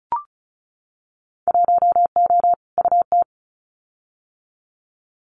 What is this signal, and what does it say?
Morse code, heard as a keyed tone, about 36 wpm, 700 Hz. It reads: 1O UT
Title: Winter Field Day Exchanges Encore 35wpm